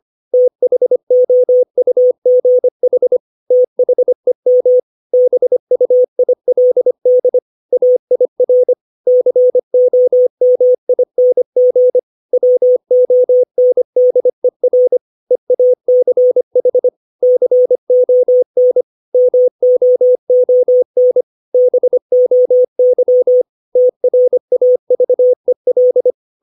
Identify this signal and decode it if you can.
Morse code, heard as a keyed tone, about 25 wpm, 500 Hz. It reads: THOUGH THEM BUILD AIR COMING WONDER EACH CON MOON BOY TRAVEL
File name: 632 words / 632 Words 25WPM